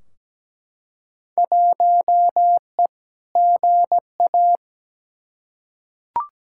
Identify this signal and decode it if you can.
Morse code, heard as a keyed tone, about 17 wpm, 700 Hz. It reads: 1E GA
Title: Field Day Exchanges - 17wpm